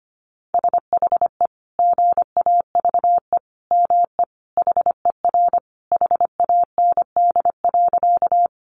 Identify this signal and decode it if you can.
Morse code, heard as a keyed tone, about 25 wpm, 700 Hz. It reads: SHE GAVE ME HER HAND.
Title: Sentences from Top 500 Words 25wpm